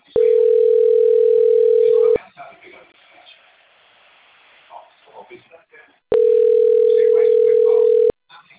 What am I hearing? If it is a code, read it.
Ringback tone (North America), 440 + 480 Hz, with speech.
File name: out- - - - - .51249